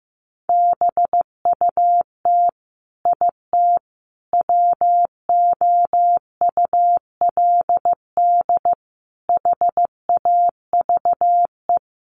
Morse code, heard as a keyed tone, about 15 wpm, 700 Hz. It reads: BUT IT WOULD HAVE
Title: Sentences from Top 1500 Words 15wpm